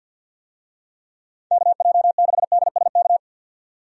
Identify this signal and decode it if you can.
Morse code, heard as a keyed tone, about 52 wpm, 700 Hz. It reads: KJ6DSK
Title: USA Call Signs - 6 Characters - 50wpm